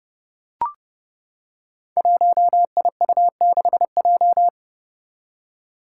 Morse code, heard as a keyed tone, about 30 wpm, 700 Hz. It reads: 1IU6J
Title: Five Challenging Letter-Number Combinations 30wpm